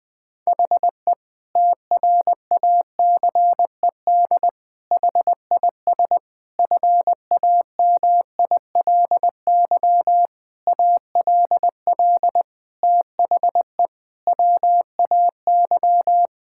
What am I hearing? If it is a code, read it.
Morse code, heard as a keyed tone, about 21 wpm, 700 Hz. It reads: HE TRACED HIS FAMILY ALL THE WAY
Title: Sentences with 6-Word Phrases - Spoken Beginning - 20wpm